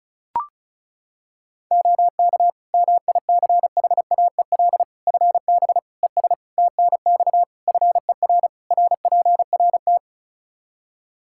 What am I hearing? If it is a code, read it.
Morse code, heard as a keyed tone, about 34 wpm, 700 Hz. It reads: OK MICHAEL FB ES TNX FER RPRT
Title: K1USN SST QSO Practice 35wpm